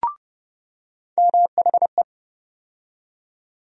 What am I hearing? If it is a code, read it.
Morse code, heard as a keyed tone, about 30 wpm, 700 Hz. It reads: MHE
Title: USA Call Sign Postfixes - 3 Letters 30wpm